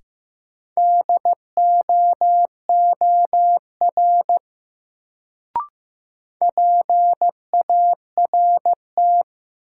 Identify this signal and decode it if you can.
Morse code, heard as a keyed tone, about 15 wpm, 700 Hz. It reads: DOOR PART
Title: Sets of 2 Words - New Word Drill - 15wpm